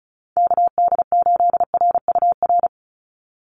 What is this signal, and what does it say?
Morse code, heard as a keyed tone, about 34 wpm, 700 Hz. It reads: KD8RUR
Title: USA Call Signs 35wpm